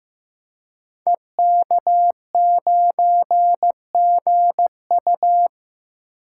Morse code, heard as a keyed tone, about 15 wpm, 700 Hz. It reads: EK9GU
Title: Code Groups - 5 Letter:Numbers - 15wpm